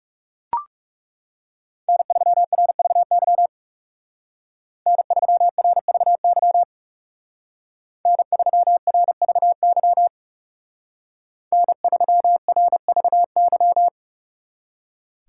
Morse code, heard as a keyed tone, about 39 wpm, 700 Hz. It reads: N3RVY N3RVY N3RVY N3RVY